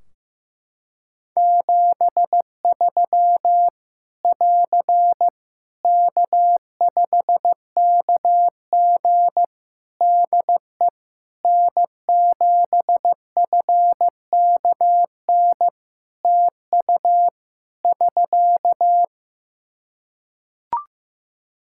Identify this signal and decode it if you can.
Morse code, heard as a keyed tone, about 15 wpm, 700 Hz. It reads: 73 <AR> K5KG DE N7FKN TU <SK>
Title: K1USN SST QSO Practice 15wpm